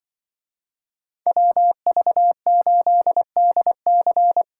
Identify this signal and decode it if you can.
Morse code, heard as a keyed tone, about 25 wpm, 700 Hz. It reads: WV8DC
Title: Speed Racing - USA 2x2 Call Signs 20wpm